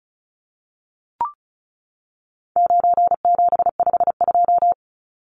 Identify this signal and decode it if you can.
Morse code, heard as a keyed tone, about 34 wpm, 700 Hz. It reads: 9752
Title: Numbers - 4-Digit 35wpm